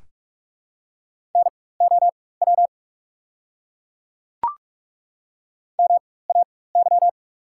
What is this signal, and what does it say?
Morse code, heard as a keyed tone, about 42 wpm, spaced as for 21 wpm, 690 Hz. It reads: NOW MAY
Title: 4x Character Spacing - Top 200 Words - 45wpm